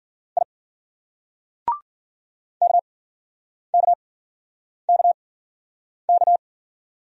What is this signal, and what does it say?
Morse code, heard as a keyed tone, about 46 wpm, spaced as for 9 wpm, 700 Hz. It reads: I KKKK
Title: Speed-Racing Single Letters 40wpm